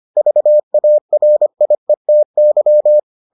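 Morse code, heard as a keyed tone, about 25 wpm, 600 Hz. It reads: VARIETY
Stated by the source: Morse edition of the Foundations of Amateur Radio episodes.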